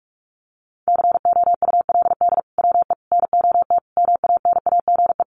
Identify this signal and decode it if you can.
Morse code, heard as a keyed tone, about 44 wpm, 700 Hz. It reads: COULD WE NOT MANAGE
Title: Sentences from Top 1400 Words 45wpm